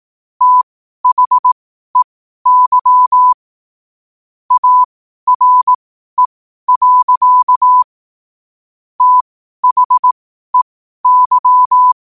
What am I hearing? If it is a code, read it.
Morse code, heard as a keyed tone, about 18 wpm, spaced as for 12 wpm, 1000 Hz. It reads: THEY ARE. THEY
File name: qotdfiles / 12wpm